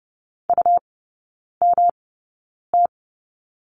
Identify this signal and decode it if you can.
Morse code, heard as a keyed tone, about 30 wpm, spaced as for 9 wpm, 700 Hz. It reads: UMT
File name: Lesson 18e - Letter m - Three Characters 30wpm